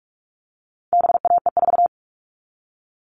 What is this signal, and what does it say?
Morse code, heard as a keyed tone, about 44 wpm, 700 Hz. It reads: BAE4
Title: All Cut Numbers - 4-Digit 45wpm